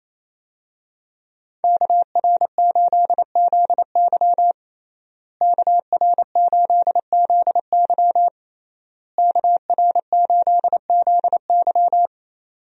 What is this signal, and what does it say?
Morse code, heard as a keyed tone, about 27 wpm, 700 Hz. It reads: KR8ZY KR8ZY KR8ZY
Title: USA Call Signs - 5 Characters - Repeated 3x - 28wpm